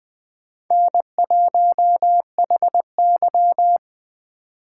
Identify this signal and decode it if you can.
Morse code, heard as a keyed tone, about 20 wpm, 700 Hz. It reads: N1HY